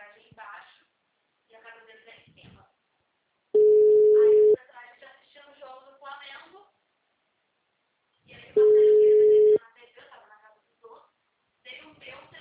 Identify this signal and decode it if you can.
Ringback tone (Europe, 425 Hz), with speech.